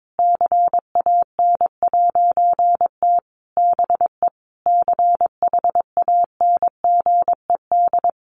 Morse code, heard as a keyed tone, about 22 wpm, 700 Hz. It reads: CAN'T BE CHANGED
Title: A Sound Of Thunder 22wpm